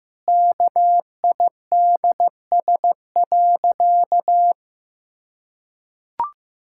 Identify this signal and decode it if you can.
Morse code, heard as a keyed tone, about 14 wpm, 700 Hz. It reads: KIDS.
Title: Pangram Challenge v2 - 15wpm